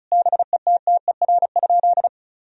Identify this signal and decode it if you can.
Morse code, heard as a keyed tone, about 34 wpm, 700 Hz. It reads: BETTER?
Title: Sentences from Top 1000 Words - Review Part 4 - 35wpm